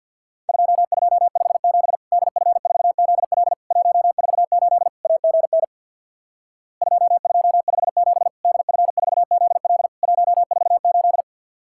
Morse code, heard as a keyed tone, about 50 wpm, 700 Hz. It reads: 2257 DF4ZL 148 AGN 2257 DF4ZL 148
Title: Worked All Europe DX Contest - QTC Exchange with Repeats - 50wpm